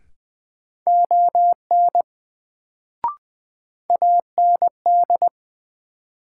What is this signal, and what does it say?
Morse code, heard as a keyed tone, about 20 wpm, 700 Hz. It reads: ON AND
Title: Sets of 2 Words - New Word Drill - 20wpm